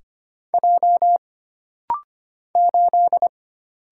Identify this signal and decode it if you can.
Morse code, heard as a keyed tone, about 25 wpm, 700 Hz. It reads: J 8
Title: Single Letter-Number - Rapid-Fire 25wpm